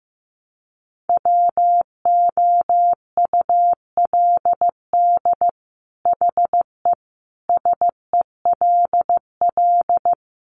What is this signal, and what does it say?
Morse code, heard as a keyed tone, about 15 wpm, 700 Hz. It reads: WOULD HE SELL
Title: Sentences from Top 1100 Words 15wpm